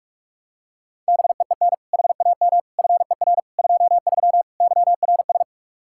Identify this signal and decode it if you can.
Morse code, heard as a keyed tone, about 46 wpm, 700 Hz. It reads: BEEN HAM FER 23 YRS
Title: Practice QSO - Exchange 3 - More Info - 45wpm